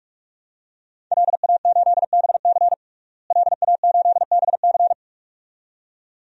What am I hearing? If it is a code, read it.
Morse code, heard as a keyed tone, about 45 wpm, 700 Hz. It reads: LA8BC LA8BC
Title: DXCC-100 Call Signs - 5 Characters - Repeated 2x - 45wpm